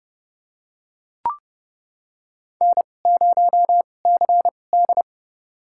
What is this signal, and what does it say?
Morse code, heard as a keyed tone, about 30 wpm, spaced as for 22 wpm, 700 Hz. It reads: N0CD
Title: USA Call Signs - 4 Characters - 2x Character Spacing - 30wpm